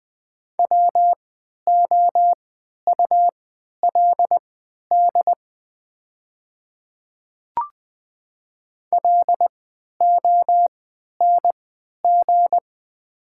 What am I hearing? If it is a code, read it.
Morse code, heard as a keyed tone, about 20 wpm, spaced as for 11 wpm, 700 Hz. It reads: WOULD LONG
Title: 3x Character Spacing - Top 200 Words - 20wpm